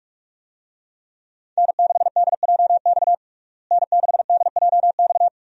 Morse code, heard as a keyed tone, about 43 wpm, 700 Hz. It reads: N6DJX N6DJX
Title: USA Call Signs - 5 Characters - Repeated 2x - 45wpm